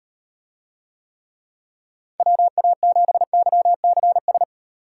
Morse code, heard as a keyed tone, about 38 wpm, 700 Hz. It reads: WA7YCS